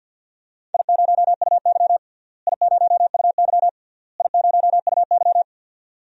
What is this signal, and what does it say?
Morse code, heard as a keyed tone, about 49 wpm, 700 Hz. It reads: I0UY I0UY I0UY
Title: DXCC-100 Call Signs - 4 Characters - Repeated 3x - 50wpm